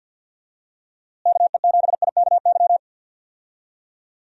Morse code, heard as a keyed tone, about 46 wpm, 700 Hz. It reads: KE7IKY
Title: USA Call Signs - 6 Characters - Repeated 2x - 50wpm